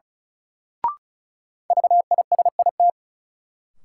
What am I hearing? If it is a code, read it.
Morse code, heard as a keyed tone, about 34 wpm, 700 Hz. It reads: VISIT
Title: Sentences from Top 900 Words - New Word Drill - 35wpm